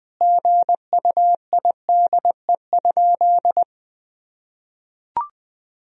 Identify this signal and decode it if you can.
Morse code, heard as a keyed tone, about 21 wpm, 700 Hz. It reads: GUIDE?
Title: Sentences from Top 1000 Words - Review Part 3 - 20wpm